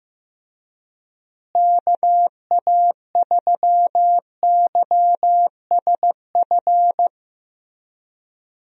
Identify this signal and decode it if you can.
Morse code, heard as a keyed tone, about 15 wpm, 700 Hz. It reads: KA3YSF
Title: USA Call Signs - 6 Characters - Repeated 3x - 15wpm